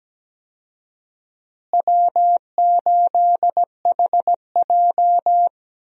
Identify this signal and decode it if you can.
Morse code, heard as a keyed tone, about 17 wpm, 700 Hz. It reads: W8HJ
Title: USA Call Signs - 4 Characters - Repeated 2x - 17wpm